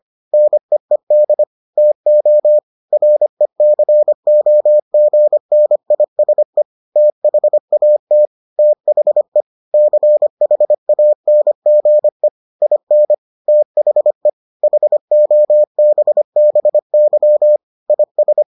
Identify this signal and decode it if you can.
Morse code, heard as a keyed tone, about 25 wpm, 600 Hz. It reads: NEED TO RECOGNISE THAT THE CHANGE IN THE HOBBY IS